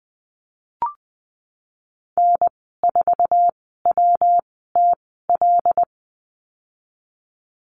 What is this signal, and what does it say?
Morse code, heard as a keyed tone, about 20 wpm, spaced as for 15 wpm, 700 Hz. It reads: N4WTL
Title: 140 Lesson 23g - Letter f - Call Signs 20wpm